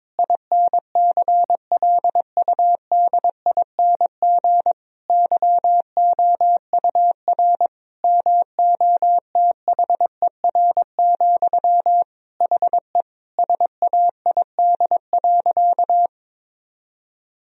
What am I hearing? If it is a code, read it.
Morse code, heard as a keyed tone, about 21 wpm, 700 Hz. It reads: INCLUDING YOUR MOTHER, HE SAID.
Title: Sentences from Top 1000 Words - Review Part 4 - 22wpm